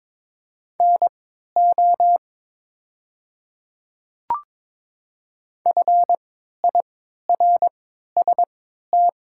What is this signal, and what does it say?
Morse code, heard as a keyed tone, about 22 wpm, spaced as for 12 wpm, 700 Hz. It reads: NO FIRST
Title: 3x Character Spacing - Top 200 Words - 22wpm